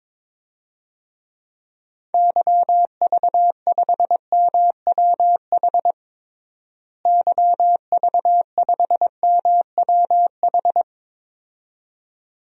Morse code, heard as a keyed tone, about 22 wpm, 700 Hz. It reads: YV5MWH YV5MWH
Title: DXCC-100 Call Signs - 6 Characters - Repeated 2x - 22wpm